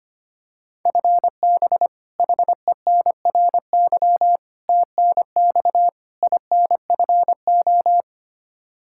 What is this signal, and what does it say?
Morse code, heard as a keyed tone, about 25 wpm, 700 Hz. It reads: FB HENRY TNX INFO
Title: Practice QSO - Exchange 3 - More Info - 25wpm